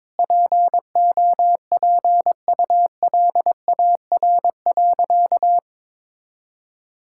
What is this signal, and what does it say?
Morse code, heard as a keyed tone, about 22 wpm, 700 Hz. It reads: POPULAR.
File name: practice / 22wpm / Sentences with 3-Word Phrases - 22wpm